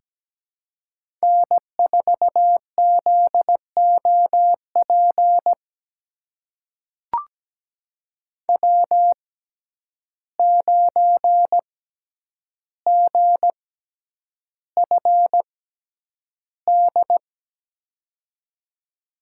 Morse code, heard as a keyed tone, about 17 wpm, spaced as for 9 wpm, 700 Hz. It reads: N4ZOP W9GFD
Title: USA Call Signs - 6x Character Spacing - 17wpm